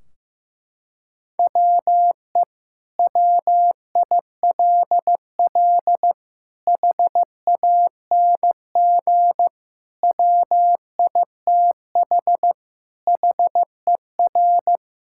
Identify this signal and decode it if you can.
Morse code, heard as a keyed tone, about 15 wpm, 700 Hz. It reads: WE WILL HANG WITH HER
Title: Sentences from Top 1600 Words 15wpm